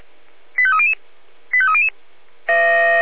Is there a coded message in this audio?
Yes: five-tone selective call (ZVEI) from 0.6 to 1.9 s, heard as quick runs of five tones, tones of about 80 ms. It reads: 85300 85300